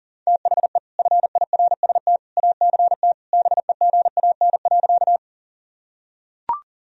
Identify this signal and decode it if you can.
Morse code, heard as a keyed tone, about 37 wpm, 700 Hz. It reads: THE FIRST ACT BEGAN.
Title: Sentences from Top 1000 Words - Review Part 1 - 40wpm